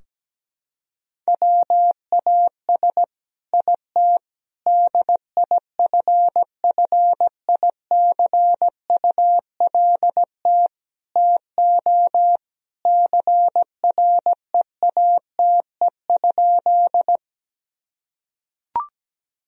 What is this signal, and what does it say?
Morse code, heard as a keyed tone, about 17 wpm, 700 Hz. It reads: WAS IT DIFFICULT TO CREATE?
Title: Sentences from Top 1000 Words - Review Part 3 - 17wpm